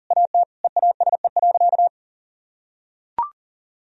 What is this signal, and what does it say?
Morse code, heard as a keyed tone, about 40 wpm, 700 Hz. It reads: AT EASE.
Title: Sentences from Top 1000 Words - Review Part 1 - 40wpm